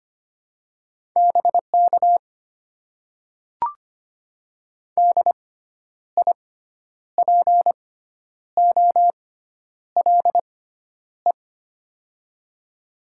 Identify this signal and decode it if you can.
Morse code, heard as a keyed tone, about 25 wpm, spaced as for 10 wpm, 700 Hz. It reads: BK DIPOLE
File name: Top 90 QSO Elements - 6x Character Spacing - 25wpm